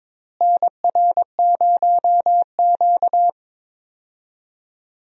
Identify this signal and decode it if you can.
Morse code, heard as a keyed tone, about 22 wpm, 700 Hz. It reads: NR0Q
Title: USA Call Signs - 4 Characters - Repeated 2x - 22wpm